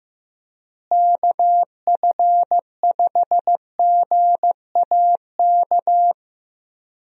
Morse code, heard as a keyed tone, about 15 wpm, 700 Hz. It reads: KF5GAK
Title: USA Call Signs - 6 Characters - Repeated 3x - 15wpm